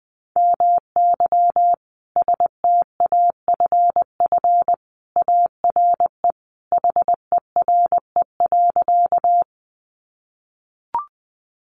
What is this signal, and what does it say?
Morse code, heard as a keyed tone, about 20 wpm, 700 Hz. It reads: MY STAFF ARE HERE.
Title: Sentences from Top 1200 Words 20wpm